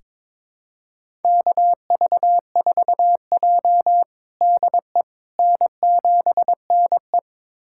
Morse code, heard as a keyed tone, about 22 wpm, 700 Hz. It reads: KV4J DE N7NE
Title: USA Callsign DE Callsign - 4 Characters - 22wpm